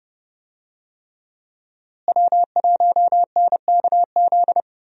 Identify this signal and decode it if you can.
Morse code, heard as a keyed tone, about 31 wpm, 700 Hz. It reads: W1NKZ
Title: USA Call Signs - 5 Characters - Repeated 3x - 30wpm